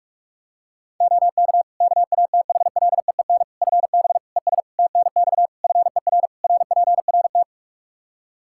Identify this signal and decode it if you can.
Morse code, heard as a keyed tone, about 46 wpm, 700 Hz. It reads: OK KATHLEEN FB ES TNX FER RPRT
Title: Practice QSO - Exchange 2 - Info - 45wpm